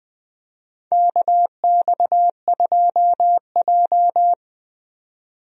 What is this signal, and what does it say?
Morse code, heard as a keyed tone, about 20 wpm, 700 Hz. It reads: KX2J
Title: USA Call Signs - 4 Characters - 20wpm